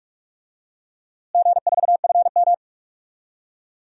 Morse code, heard as a keyed tone, about 44 wpm, 700 Hz. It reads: G4FM
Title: DXCC-100 Call Signs - 4 Characters - Repeated 2x - 45wpm